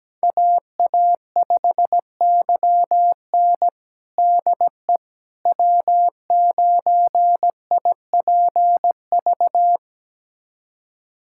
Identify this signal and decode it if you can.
Morse code, heard as a keyed tone, about 17 wpm, 700 Hz. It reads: AA5YN DE W9IPV
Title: USA Callsign DE Callsign - 5 Characters - 17wpm